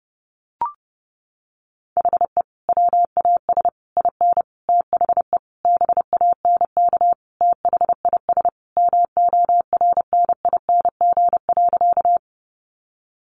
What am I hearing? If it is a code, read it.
Morse code, heard as a keyed tone, about 30 wpm, 700 Hz. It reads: HE WAS IN THE BANK THIS MORNING.
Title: Sentences from Top 900 Words 30wpm